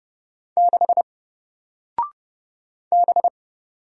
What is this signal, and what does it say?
Morse code, heard as a keyed tone, about 30 wpm, 700 Hz. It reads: 6 B
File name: Single-Letter-Number-30wpm